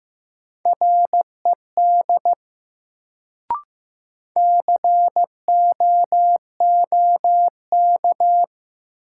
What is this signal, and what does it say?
Morse code, heard as a keyed tone, about 15 wpm, 700 Hz. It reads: RED COOK
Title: Sentences from Top 700 Words - New Word Drill - 15wpm